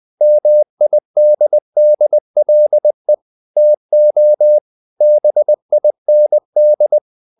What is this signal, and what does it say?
Morse code, heard as a keyed tone, about 20 wpm, 600 Hz. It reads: MIDDLE TO BIND
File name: the old man and the sea part 2 20wpm